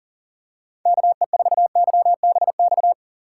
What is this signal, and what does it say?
Morse code, heard as a keyed tone, about 40 wpm, 700 Hz. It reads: KE4YBX
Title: USA Call Signs - 6 Characters - 2x Character Spacing - 40wpm